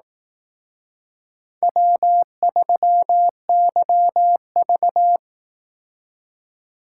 Morse code, heard as a keyed tone, about 18 wpm, 700 Hz. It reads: W3YV
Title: Speed Racing - USA Call Signs 15wpm